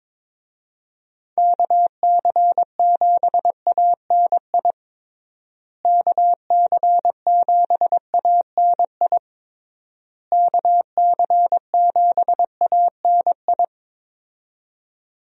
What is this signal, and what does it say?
Morse code, heard as a keyed tone, about 22 wpm, 700 Hz. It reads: KC7ANI KC7ANI KC7ANI
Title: USA Call Signs - 6 Characters - Repeated 3x - 22wpm